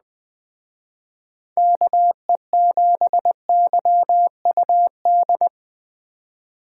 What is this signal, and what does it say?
Morse code, heard as a keyed tone, about 20 wpm, 700 Hz. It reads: KE7YUD
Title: Speed Racing - USA 2x3 Call Signs 17wpm